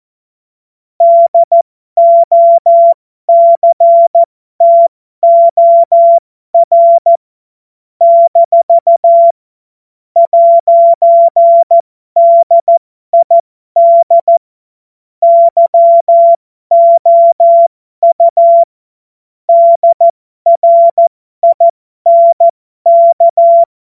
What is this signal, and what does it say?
Morse code, heard as a keyed tone, about 14 wpm, spaced as for 12 wpm, 680 Hz. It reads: DOCTOR - 'DID YOU DRINK
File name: Morsekurs / Witze